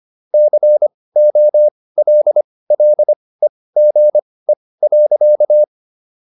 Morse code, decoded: COLLEGE.

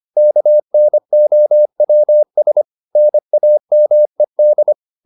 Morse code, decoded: KNOWS NAMED

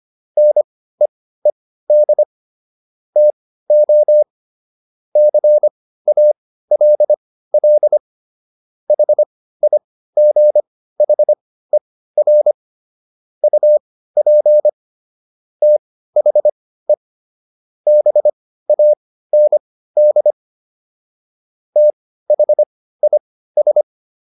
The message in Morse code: NEED TO CALL HIGHER UP THE BAND THIS

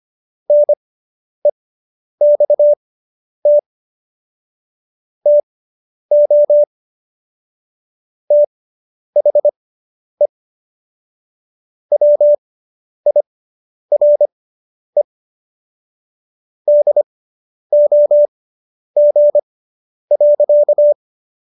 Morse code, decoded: NEXT TO THE WIRE DOG.

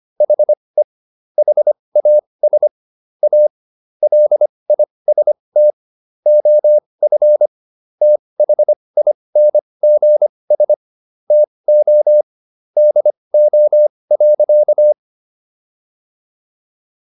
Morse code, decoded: HE HAS A LIST OF THINGS TO DO.

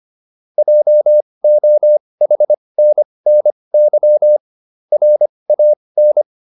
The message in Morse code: JOHNNY RAN

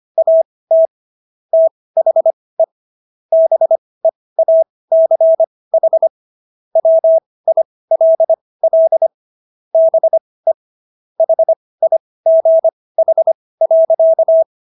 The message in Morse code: AT THE BEACH WILL BE HIGH.

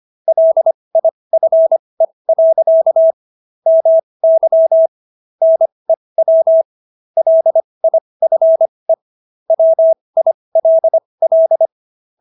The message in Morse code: LIFE. MY NEW LIFE WILL

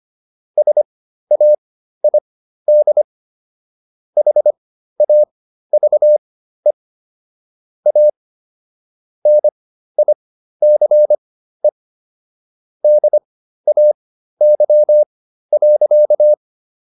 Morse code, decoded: SAID HAVE A NICE DAY.

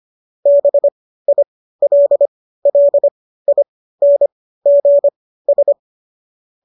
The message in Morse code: BILLINGS

